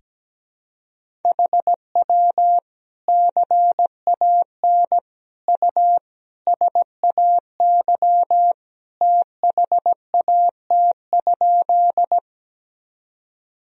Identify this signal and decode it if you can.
Morse code, heard as a keyed tone, about 17 wpm, 700 Hz. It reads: HW CAN U SAY THAT?
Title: Sentences from Top 200 Words and Abbreviations 17wpm